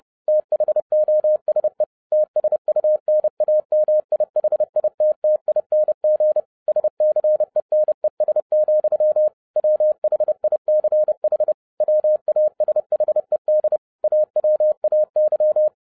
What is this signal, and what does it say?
Morse code, heard as a keyed tone, about 30 wpm, 610 Hz. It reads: THOSE TSUNAMI HITTING SCENES, WHICH WASHED AWAY